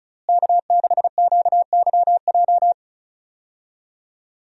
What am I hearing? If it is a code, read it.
Morse code, heard as a keyed tone, about 36 wpm, 700 Hz. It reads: K6QYJ
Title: USA Call Signs - 5 Characters - Repeated 3x - 35wpm